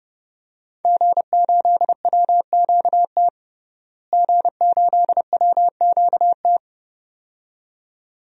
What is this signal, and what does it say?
Morse code, heard as a keyed tone, about 30 wpm, 700 Hz. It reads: G8WQT G8WQT
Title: DXCC-100 Call Signs - 5 Characters - Repeated 2x - 30wpm